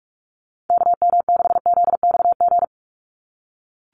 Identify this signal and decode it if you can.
Morse code, heard as a keyed tone, about 44 wpm, 700 Hz. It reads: KM6ZXG
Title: USA Call Signs 45wpm